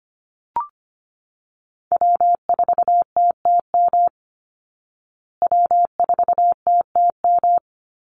Morse code, heard as a keyed tone, about 25 wpm, 700 Hz. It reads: W4TTM W4TTM
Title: Call Signs Repeated 3x - 25wpm